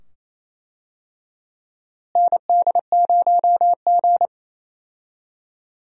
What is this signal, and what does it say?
Morse code, heard as a keyed tone, about 27 wpm, 700 Hz. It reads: ND0G
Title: USA Call Signs - 4 Characters - Repeated 2x - 28wpm